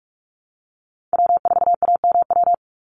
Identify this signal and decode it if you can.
Morse code, heard as a keyed tone, about 44 wpm, 700 Hz. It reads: W4AMW
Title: NA SOTA Chaser Callsigns 45wpm